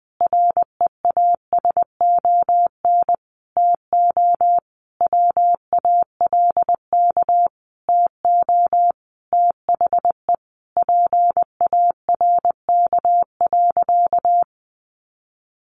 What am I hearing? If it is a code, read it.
Morse code, heard as a keyed tone, about 20 wpm, 700 Hz. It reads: REASON TO WALK TO THE PARK.
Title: Sentences from Top 1400 Words 20wpm